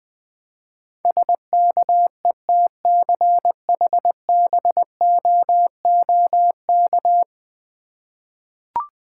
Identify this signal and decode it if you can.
Morse code, heard as a keyed tone, about 20 wpm, 700 Hz. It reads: SKETCHBOOK
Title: Difficult Words with Hints - 20wpm